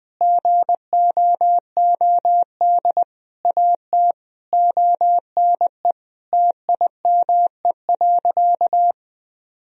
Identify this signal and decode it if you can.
Morse code, heard as a keyed tone, about 20 wpm, 700 Hz. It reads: GOOD AT ONE TIME.
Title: Sentences from Top 1000 Words - Review Part 2 - 20wpm